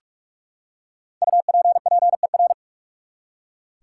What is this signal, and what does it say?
Morse code, heard as a keyed tone, about 46 wpm, 700 Hz. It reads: UPPER
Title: Five Letter Words - 45wpm